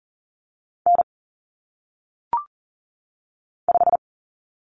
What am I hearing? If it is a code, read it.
Morse code, heard as a keyed tone, about 40 wpm, 700 Hz. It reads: N 5
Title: Common Cut Numbers - 1-Digit 40wpm